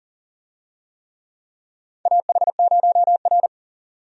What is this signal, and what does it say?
Morse code, heard as a keyed tone, about 40 wpm, 700 Hz. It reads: AH0R
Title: USA Call Signs - 4 Characters - 40wpm